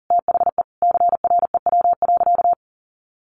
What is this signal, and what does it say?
Morse code, heard as a keyed tone, about 40 wpm, 700 Hz. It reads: THE CREW.
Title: Sentences from Top 1500 Words 40wpm